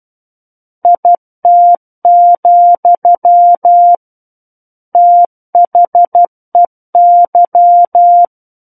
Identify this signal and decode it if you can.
Morse code, heard as a keyed tone, about 12 wpm, 700 Hz. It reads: IT, THEY
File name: Book One / 03-Chapter 3 - 12 WPM